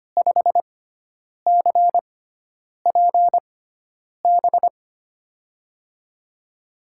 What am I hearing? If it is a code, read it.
Morse code, heard as a keyed tone, about 25 wpm, spaced as for 9 wpm, 700 Hz. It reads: 5CPB
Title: USA Call Signs - 6 Characters - 6x Character Spacing - 25wpm